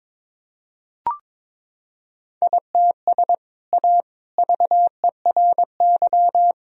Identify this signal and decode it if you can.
Morse code, heard as a keyed tone, about 22 wpm, 700 Hz. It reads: ITS A VERY
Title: Sentences from Top 1000 Words - Review Part 3 - 22wpm